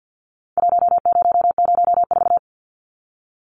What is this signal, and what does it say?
Morse code, heard as a keyed tone, about 50 wpm, 700 Hz. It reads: 1004
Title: Numbers - Contest Style - 4-Digit 50wpm